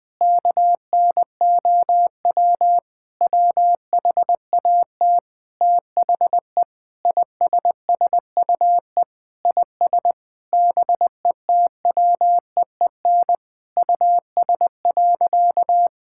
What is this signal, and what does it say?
Morse code, heard as a keyed tone, about 20 wpm, 700 Hz. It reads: KNOW WHAT THE ISSUE IS BETWEEN US.
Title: Sentences from Top 1000 Words - Review Part 6 - 20wpm